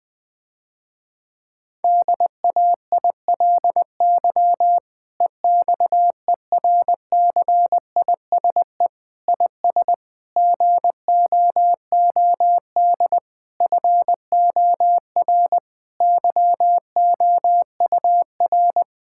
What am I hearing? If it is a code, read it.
Morse code, heard as a keyed tone, about 21 wpm, 700 Hz. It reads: DAILY EXERCISE IS GOOD FOR YOUR
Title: Difficult Sentences with Hints - 20wpm